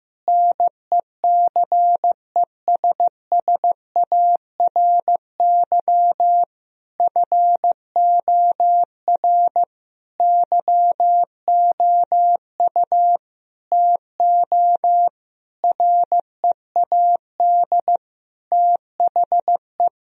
Morse code, heard as a keyed tone, about 15 wpm, 700 Hz. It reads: NECESSARY FOR YOU TO READ THE
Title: Sentences from Top 1000 Words - Review Part 1 - 15wpm